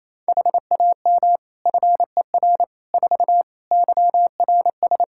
Morse code, heard as a keyed tone, about 28 wpm, 700 Hz. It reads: HAM FER 4 YRS
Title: Practice QSO - Exchange 3 - More Info - 28wpm